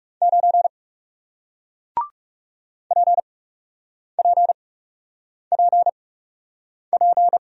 Morse code, heard as a keyed tone, about 38 wpm, spaced as for 8 wpm, 700 Hz. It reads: 9 PPPP